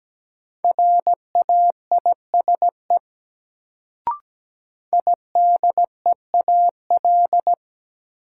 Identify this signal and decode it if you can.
Morse code, heard as a keyed tone, about 17 wpm, 700 Hz. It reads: RAISE IDEAL
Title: Sentences from Top 1400 Words - New Word Drill Alt - 17wpm